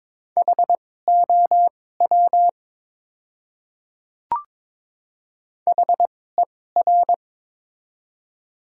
Morse code, heard as a keyed tone, about 23 wpm, spaced as for 16 wpm, 700 Hz. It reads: HOW HER
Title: 2x Character Spacing - Top 100 Words - 22wpm